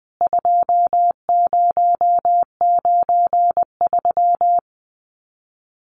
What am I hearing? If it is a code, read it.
Morse code, heard as a keyed tone, about 20 wpm, 700 Hz. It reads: 2093
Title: CWT Exchange 20wpm